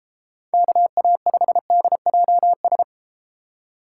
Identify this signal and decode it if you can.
Morse code, heard as a keyed tone, about 33 wpm, 700 Hz. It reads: KA5DJS